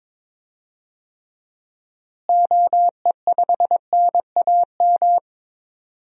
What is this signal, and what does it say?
Morse code, heard as a keyed tone, about 22 wpm, 700 Hz. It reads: OE5NAM
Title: DXCC-100 Call Signs - 6 Characters - Repeated 2x - 22wpm